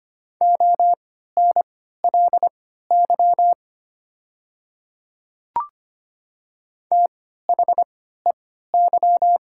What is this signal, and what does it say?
Morse code, heard as a keyed tone, about 25 wpm, spaced as for 14 wpm, 700 Hz. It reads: ONLY THEY
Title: 3x Character Spacing - Top 500 Words - 25wpm